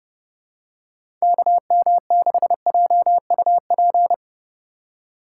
Morse code, heard as a keyed tone, about 32 wpm, 700 Hz. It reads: KM6JUP
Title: USA Call Signs - 6 Characters - 4x Character Spacing - 30wpm